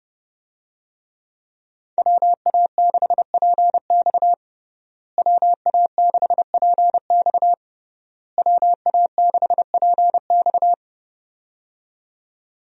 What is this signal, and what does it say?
Morse code, heard as a keyed tone, about 30 wpm, 700 Hz. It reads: WA6PX WA6PX WA6PX
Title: USA Call Signs - 5 Characters - Repeated 3x - 30wpm